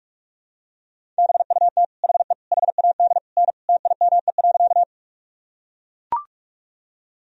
Morse code, heard as a keyed tone, about 41 wpm, 700 Hz. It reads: BUT HE HAD N TIME.
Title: Sentences from Top 100 Words and Abbreviations 45wpm